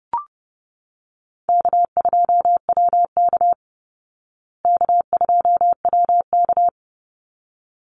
Morse code, heard as a keyed tone, about 30 wpm, 700 Hz. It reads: K2WK K2WK
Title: Easy Call Signs Repeated 2x - 30wpm